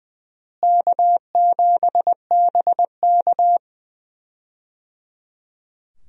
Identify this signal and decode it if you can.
Morse code, heard as a keyed tone, about 21 wpm, 700 Hz. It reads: K7BK
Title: USA Call Signs - 4 Characters - Repeated 3x - 20wpm